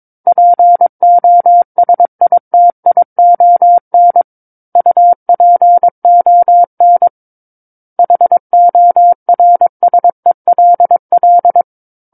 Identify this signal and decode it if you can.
Morse code, heard as a keyed tone, about 22 wpm, 700 Hz. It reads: POSITION UPON HORSELL
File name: Book One / 13-Chapter 13 - 22 WPM